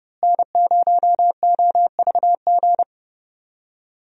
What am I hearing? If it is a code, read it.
Morse code, heard as a keyed tone, about 30 wpm, 700 Hz. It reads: N0OVG
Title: USA Call Signs - 5 Characters - Repeated 3x - 30wpm